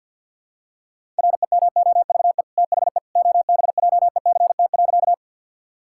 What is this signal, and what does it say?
Morse code, heard as a keyed tone, about 52 wpm, 700 Hz. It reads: REMOVE THE OBJECT.
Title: Sentences from Top 1000 Words - Review Part 5 - 50wpm